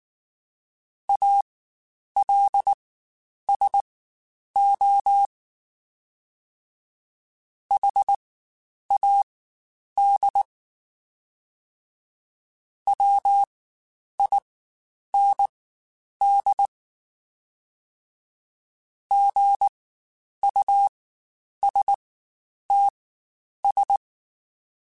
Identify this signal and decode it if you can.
Morse code, heard as a keyed tone, about 19 wpm, spaced as for 9 wpm, 800 Hz. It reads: ALSO HAD WIND GUSTS
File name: Spaced Characters / Spaced Characters-mp3s / Weather5 5 sp @18